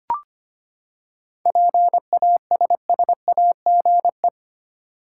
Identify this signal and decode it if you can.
Morse code, heard as a keyed tone, about 25 wpm, 700 Hz. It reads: PASSAGE
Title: Seven Letter Words - 25wpm